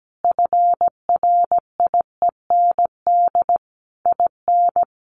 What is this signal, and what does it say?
Morse code, heard as a keyed tone, about 17 wpm, 700 Hz. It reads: FRIEND IN
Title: Sentences from Top 1400 Words 17wpm